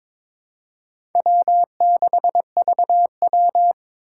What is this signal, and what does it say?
Morse code, heard as a keyed tone, about 23 wpm, 700 Hz. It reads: W6VW
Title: USA Call Signs - 4 Characters - 4x Character Spacing - 22wpm